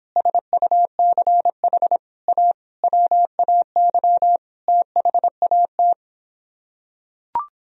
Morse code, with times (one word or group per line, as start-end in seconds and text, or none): 0.2-2.0 SUCH
2.3-2.5 A
2.8-4.4 WAY
4.7-5.9 THAT